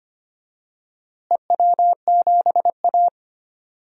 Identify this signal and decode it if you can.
Morse code, heard as a keyed tone, about 25 wpm, 700 Hz. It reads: EW7A
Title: DXCC-100 Call Signs - 4 Characters - Repeated 2x - 25wpm